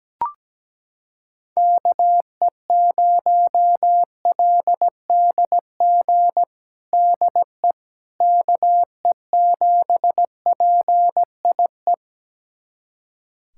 Morse code, heard as a keyed tone, about 17 wpm, 700 Hz. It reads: KE0LDG DE KE7PIE
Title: USA Callsign DE Callsign - 6 Characters - 17wpm